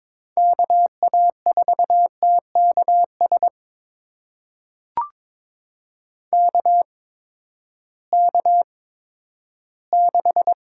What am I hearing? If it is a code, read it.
Morse code, heard as a keyed tone, about 22 wpm, spaced as for 13 wpm, 700 Hz. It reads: KA4TKS KK6
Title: USA Call Signs - 8x Character Spacing - 22wpm